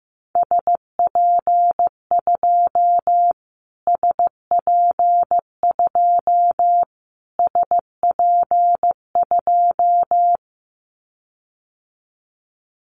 Morse code, heard as a keyed tone, about 15 wpm, 700 Hz. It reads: SP2 SP2 SP2
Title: Callsign Prefixes for Top 25 DXCC - Repeated 3x - 15wpm